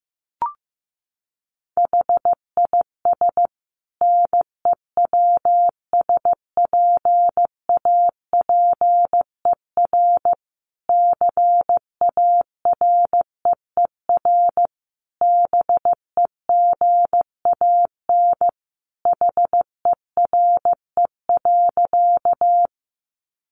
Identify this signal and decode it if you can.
Morse code, heard as a keyed tone, about 14 wpm, 700 Hz. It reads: HIS NEWSPAPER CAREER BEGAN HERE.
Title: Sentences from Top 1100 Words 15wpm